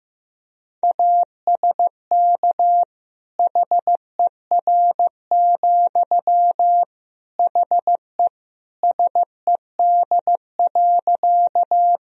Morse code, heard as a keyed tone, about 15 wpm, 700 Hz. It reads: ASK HER, HE SED.
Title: Sentences from Top 200 Words and Abbreviations 15wpm